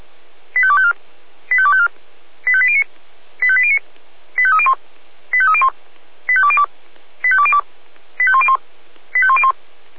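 Five-tone selective call (ZVEI), heard as quick runs of five tones, tones of about 70 ms. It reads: 85254 85254 85809 85809 85301 85301 85202 85202 85101 85101
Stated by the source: Five-tone ZVEI selective call received by SDR, from audio server of Feuerwehr Orlamünde (fire brigade).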